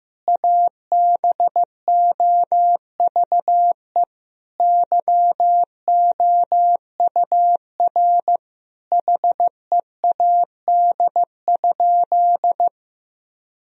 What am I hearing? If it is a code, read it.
Morse code, heard as a keyed tone, about 15 wpm, 700 Hz. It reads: ABOVE YOUR HEAD?
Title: Sentences from Top 1000 Words - Review Part 3 - 15wpm